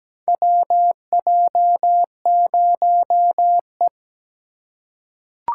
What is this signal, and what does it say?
Morse code, heard as a keyed tone, about 17 wpm, 700 Hz. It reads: WJ0E
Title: USA Call Signs - 4 Characters - 8x Character Spacing - 17wpm